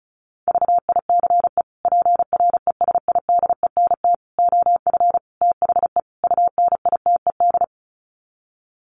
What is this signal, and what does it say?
Morse code, heard as a keyed tone, about 34 wpm, 700 Hz. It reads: VICE PRESIDENT OF THE UNITED
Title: Sets of 5 Words Encore2 35wpm